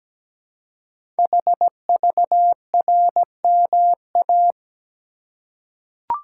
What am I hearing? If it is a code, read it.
Morse code, heard as a keyed tone, about 17 wpm, 700 Hz. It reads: HVRMA
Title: Code Groups - 5 Letters - 17wpm